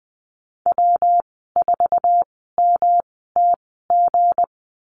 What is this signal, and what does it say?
Morse code, heard as a keyed tone, about 20 wpm, spaced as for 14 wpm, 700 Hz. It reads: W4MTG
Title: 167 Lesson 27g - Number 7 - Call Signs 20wpm